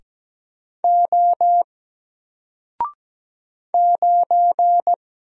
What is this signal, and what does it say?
Morse code, heard as a keyed tone, about 17 wpm, 700 Hz. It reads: O 9
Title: Single Character (Letter:Number:Punctuation:Prosign) Alt 17wpm